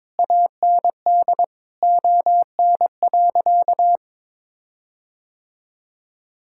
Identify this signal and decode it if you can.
Morse code, heard as a keyed tone, about 21 wpm, 700 Hz. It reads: AND ON.
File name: Sentences with 7-Word Phrases - 22wpm